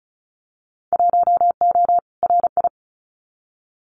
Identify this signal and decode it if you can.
Morse code, heard as a keyed tone, about 34 wpm, 700 Hz. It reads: 1O RI
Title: Winter Field Day Exchanges 35wpm